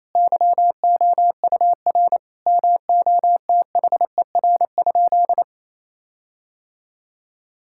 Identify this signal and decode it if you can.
Morse code, heard as a keyed tone, about 28 wpm, 700 Hz. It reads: YOUR MOTHER?
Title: Sentences from Top 1600 Words 28wpm